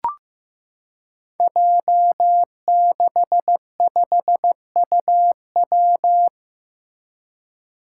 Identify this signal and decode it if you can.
Morse code, heard as a keyed tone, about 15 wpm, 700 Hz. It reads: J65UW
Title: Five Challenging Letter-Number Combinations 15wpm